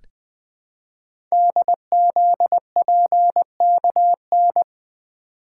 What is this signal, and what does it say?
Morse code, heard as a keyed tone, about 20 wpm, 700 Hz. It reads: DZPKN
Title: Code Groups - 5 Letters - 20wpm